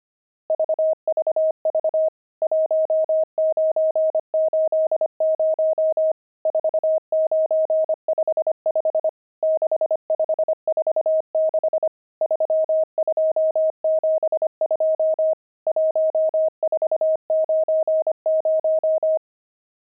Morse code, heard as a keyed tone, about 25 wpm, 630 Hz. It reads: VVV 1980 4955 6546 3272 1490
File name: pangram-fs-25-cs-25-nums8